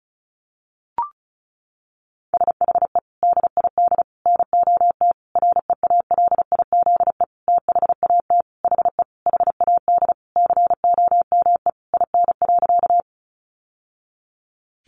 Morse code, heard as a keyed tone, about 34 wpm, 700 Hz. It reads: SHE DID NOT REALIZE THAT HE HAD COME IN.
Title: Sentences from Top 1000 Words 35wpm